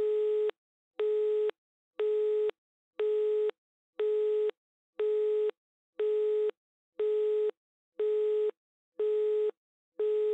Busy tone (Europe, 425 Hz).
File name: NS Studios - ultra sound collection / devices accessories utilities / phone busy signal3